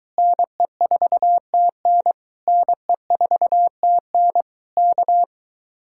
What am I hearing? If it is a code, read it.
Morse code, heard as a keyed tone, about 24 wpm, 700 Hz. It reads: NE4TN NE4TN K